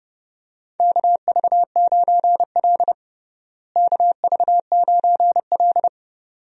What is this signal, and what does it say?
Morse code, heard as a keyed tone, about 30 wpm, 700 Hz. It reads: KV9L KV9L
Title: USA Call Signs - 4 Characters - Repeated 2x - 30wpm